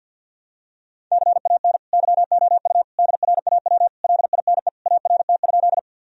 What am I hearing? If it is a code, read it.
Morse code, heard as a keyed tone, about 52 wpm, 700 Hz. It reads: CAN YOU DRAW LINE ART?
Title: Sentences from Top 1000 Words - Review Part 3 - 50wpm